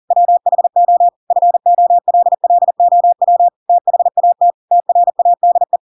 Morse code, heard as a keyed tone, about 40 wpm, 700 Hz. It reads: WHO FOLLOW THAT TRADE
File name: Pinocchio / Pinocchio ch4 40 40 ik1hge